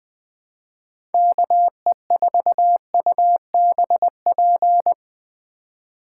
Morse code, heard as a keyed tone, about 20 wpm, 700 Hz. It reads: KE4UBP
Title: Speed Racing - USA 2x3 Call Signs 17wpm